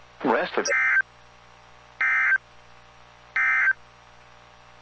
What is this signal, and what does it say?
Emergency Alert System (SAME) tones, heard as harsh buzzing bursts, with speech. It is heard from 0.7 to 3.8 s.